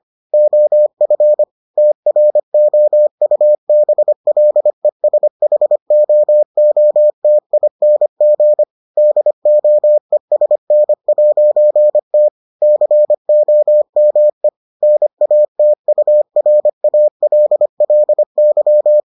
Morse code, heard as a keyed tone, about 25 wpm, 600 Hz. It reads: OF TROUBLESHOOTING DOESN'T COME NATURALLY